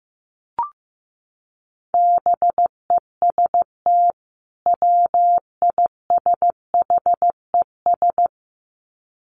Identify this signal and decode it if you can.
Morse code, heard as a keyed tone, about 15 wpm, 700 Hz. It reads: BEST WISHES
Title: Common QSO Phrases 15wpm